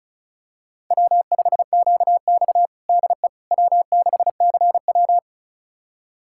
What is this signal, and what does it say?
Morse code, heard as a keyed tone, about 36 wpm, 700 Hz. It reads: W5QX DE W6CW
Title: USA Callsign DE Callsign - 4 Characters - 35wpm